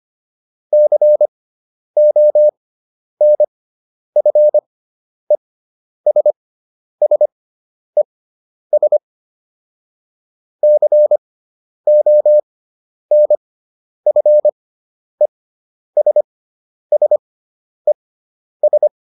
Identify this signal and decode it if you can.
Morse code, heard as a keyed tone, about 25 wpm, spaced as for 10 wpm, 600 Hz. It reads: CONFESSES CONFESSES